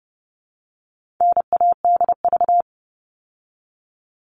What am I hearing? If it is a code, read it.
Morse code, heard as a keyed tone, about 30 wpm, 700 Hz. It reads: NADV
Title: All Cut Numbers - 4-Digit 30wpm